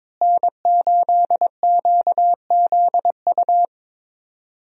Morse code, heard as a keyed tone, about 22 wpm, 700 Hz. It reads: N8QZU
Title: USA Call Signs - 5 Characters - Repeated 3x - 22wpm